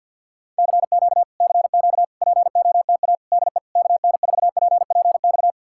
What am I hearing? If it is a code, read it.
Morse code, heard as a keyed tone, about 66 wpm, spaced as for 58 wpm, 700 Hz. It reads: CQ CQ POTA DE KN4PPX
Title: Speed-Racing Common QSO Phrases - 50wpm